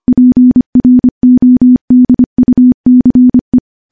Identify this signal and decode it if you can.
Morse code, heard as a keyed tone, about 25 wpm, 250 Hz. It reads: PRODUCE
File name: morse / morse-training-25WPM